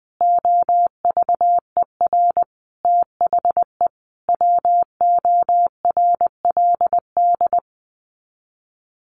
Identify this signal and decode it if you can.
Morse code, heard as a keyed tone, about 20 wpm, 700 Hz. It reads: OVER THE WORLD
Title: Sets of 4 Words 20wpm